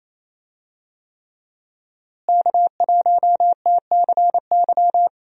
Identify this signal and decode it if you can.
Morse code, heard as a keyed tone, about 27 wpm, 700 Hz. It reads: K1TCY
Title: USA Call Signs - 5 Characters - Repeated 3x - 28wpm